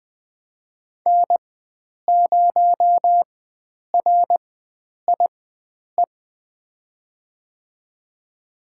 Morse code, heard as a keyed tone, about 20 wpm, spaced as for 9 wpm, 700 Hz. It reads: N0RIE
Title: USA Call Signs - 5 Characters - 4x Character Spacing - 20wpm